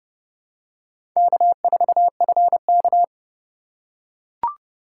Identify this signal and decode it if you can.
Morse code, heard as a keyed tone, about 28 wpm, 700 Hz. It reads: K4FK